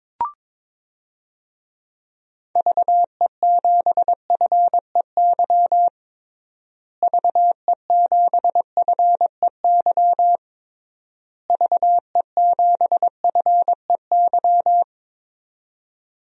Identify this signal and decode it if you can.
Morse code, heard as a keyed tone, about 22 wpm, 700 Hz. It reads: VE7FEY VE7FEY VE7FEY
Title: DXCC-100 Call Signs - 6 Characters - Repeated 3x - 22wpm